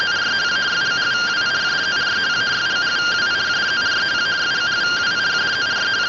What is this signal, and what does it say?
Two-tone teleprinter signal, about 170 Hz shift.